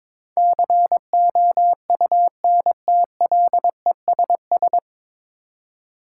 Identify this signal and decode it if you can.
Morse code, heard as a keyed tone, about 22 wpm, 700 Hz. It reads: COUNTLESS
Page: Morse Code Ninja practice 22wpm